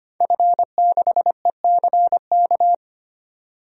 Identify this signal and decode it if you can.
Morse code, heard as a keyed tone, about 25 wpm, 700 Hz. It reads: F6ECK
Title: DXCC-100 Call Signs - 5 Characters - 25wpm